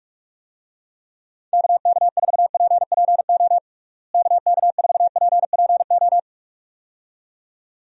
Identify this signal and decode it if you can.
Morse code, heard as a keyed tone, about 43 wpm, 700 Hz. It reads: KK4PPO KK4PPO
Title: USA Call Signs - 6 Characters - Repeated 2x - 45wpm